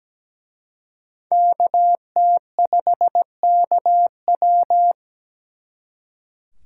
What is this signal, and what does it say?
Morse code, heard as a keyed tone, about 21 wpm, 700 Hz. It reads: KT5KW
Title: Speed Racing - USA Call Signs 17wpm